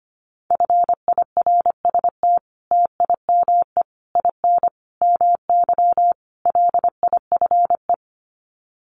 Morse code, heard as a keyed tone, about 25 wpm, 700 Hz. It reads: FIRST TIME IN MY LIFE
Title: Sets of 5 Words Encore2 25wpm